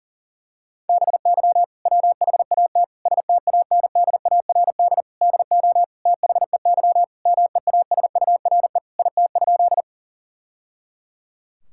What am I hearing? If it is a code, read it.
Morse code, heard as a keyed tone, about 37 wpm, 700 Hz. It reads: BY WHAT STANDARD DO THEY MEASURE IT?
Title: Sentences from Top 1000 Words - Review Part 6 - 40wpm